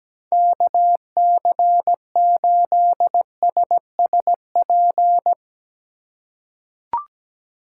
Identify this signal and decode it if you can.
Morse code, heard as a keyed tone, about 17 wpm, 700 Hz. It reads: KC8SSP
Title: USA Call Signs - 6 Characters - 2x Character Spacing - 17wpm